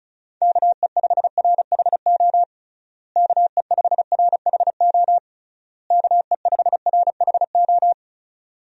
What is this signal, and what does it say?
Morse code, heard as a keyed tone, about 36 wpm, 700 Hz. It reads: KE5RHO KE5RHO KE5RHO
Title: USA Call Signs - 6 Characters - Repeated 3x - 35wpm